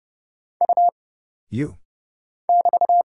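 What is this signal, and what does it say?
Morse code, heard as a keyed tone, about 30 wpm, 700 Hz. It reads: U <BT>
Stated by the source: Morse Code Ninja single-character practice (letters, numbers, punctuation, prosigns) at 30 wpm, by Kurt Zoglmann (AD0WE).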